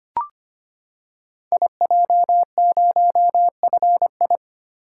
Morse code, heard as a keyed tone, about 25 wpm, 700 Hz. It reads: IJ0FI
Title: Code Groups - 5 Letter:Numbers - 25wpm